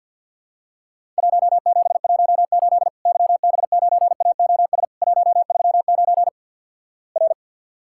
Morse code, heard as a keyed tone, about 48 wpm, 700 Hz. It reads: 1718 YB9AOS 139 R
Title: Worked All Europe DX Contest - QTC Exchange with Repeats - 50wpm